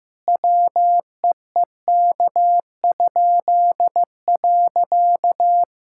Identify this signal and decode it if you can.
Morse code, heard as a keyed tone, about 15 wpm, 700 Hz. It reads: WEEK?.
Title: Pangram Challenge v2 - 15wpm